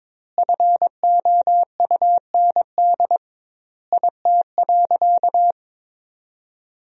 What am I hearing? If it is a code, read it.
Morse code, heard as a keyed tone, about 22 wpm, 700 Hz. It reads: FOUND IT.
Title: 2x Word-Spacing - Sentences from Top 300 Words - 22wpm